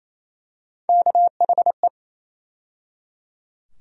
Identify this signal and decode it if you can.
Morse code, heard as a keyed tone, about 27 wpm, 700 Hz. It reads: KHE
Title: USA Call Sign Postfixes - 3 Letters 28wpm